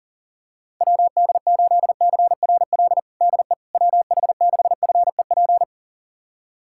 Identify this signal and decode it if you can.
Morse code, heard as a keyed tone, about 40 wpm, 700 Hz. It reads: WD8CRL DE WH6FEP
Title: USA Callsign DE Callsign - 6 Characters - 40wpm